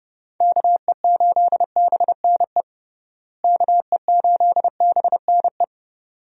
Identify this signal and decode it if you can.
Morse code, heard as a keyed tone, about 30 wpm, 700 Hz. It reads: KE8BNE KE8BNE
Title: USA Call Signs - 6 Characters - Repeated 2x - 30wpm